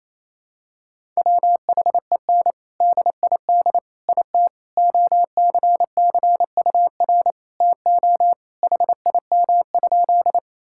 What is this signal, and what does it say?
Morse code, heard as a keyed tone, about 28 wpm, 700 Hz. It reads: WHEN DID IT OCCUR TO HIM?
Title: Sentences from Top 1000 Words - Review Part 6 - 28wpm